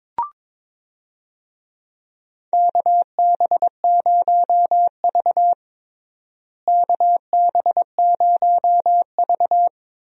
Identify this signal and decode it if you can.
Morse code, heard as a keyed tone, about 22 wpm, 700 Hz. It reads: KB0V KB0V
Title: USA Call Signs - 4 Characters - Repeated 3x - 22wpm